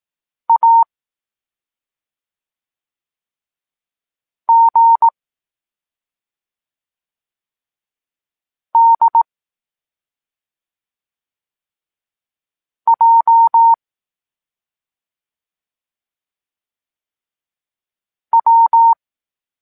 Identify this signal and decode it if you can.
Morse code, heard as a keyed tone, about 18 wpm, 920 Hz. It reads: A G D J W
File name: Tekst-23 4